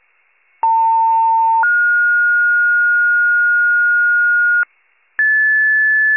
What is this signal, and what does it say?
Two-tone page, heard as a one-second tone, then a three-second tone, about 900 then 1430 Hz, with noise. It is heard from 0.6 to 4.6 s.
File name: pager / COUNTY EMS / wcalert